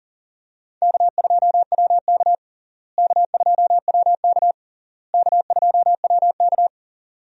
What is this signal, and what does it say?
Morse code, heard as a keyed tone, about 40 wpm, 700 Hz. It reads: K2WK K2WK K2WK
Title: USA Call Signs - 4 Characters - Repeated 3x - 40wpm